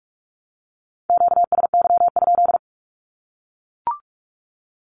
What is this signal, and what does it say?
Morse code, heard as a keyed tone, about 44 wpm, 700 Hz. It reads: QSY?
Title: Q-Codes 45wpm